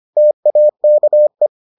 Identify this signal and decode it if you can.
Morse code, heard as a keyed tone, about 25 wpm, 600 Hz. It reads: TAKE